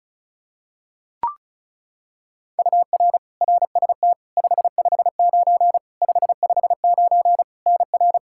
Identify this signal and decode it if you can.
Morse code, heard as a keyed tone, about 34 wpm, 700 Hz. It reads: UR RST 559 559 NR
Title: Practice QSO - Exchange 1 - Beginning - 35wpm